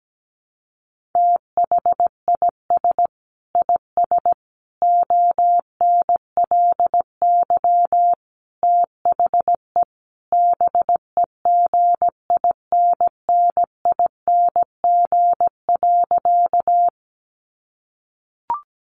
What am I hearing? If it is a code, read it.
Morse code, heard as a keyed tone, about 17 wpm, 700 Hz. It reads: THIS IS ONLY THE BEGINNING.
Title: Sentences from Top 800 Words 17wpm